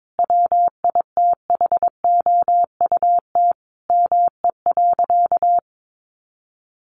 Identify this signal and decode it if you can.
Morse code, heard as a keyed tone, about 22 wpm, 700 Hz. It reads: WITHOUT ME.
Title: Sentences from Top 1200 Words 22wpm